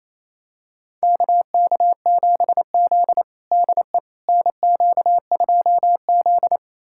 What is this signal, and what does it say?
Morse code, heard as a keyed tone, about 28 wpm, 700 Hz. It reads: KK7Z DE NQ2Z
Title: USA Callsign DE Callsign - 4 Characters - 28wpm